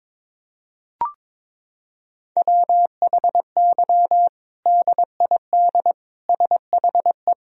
Morse code, heard as a keyed tone, about 22 wpm, 700 Hz. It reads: WHY DID SHE
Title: Sentences from Top 1000 Words - Review Part 1 - 22wpm